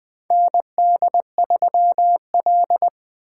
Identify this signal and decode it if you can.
Morse code, heard as a keyed tone, about 20 wpm, 700 Hz. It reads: ND3L
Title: K1USN SST QSO Practice - 20wpm